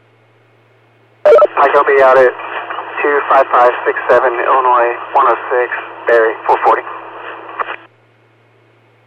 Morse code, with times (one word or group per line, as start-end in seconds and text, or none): none